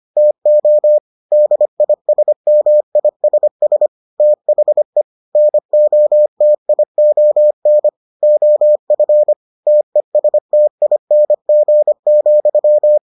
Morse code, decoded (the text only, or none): TO DISMISS THE NOTION OF TESTING,